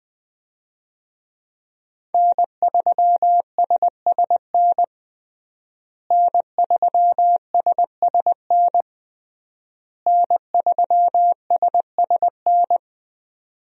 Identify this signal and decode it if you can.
Morse code, heard as a keyed tone, about 20 wpm, 700 Hz. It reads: N3SSN N3SSN N3SSN
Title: USA Call Signs - 5 Characters - Repeated 3x - 20wpm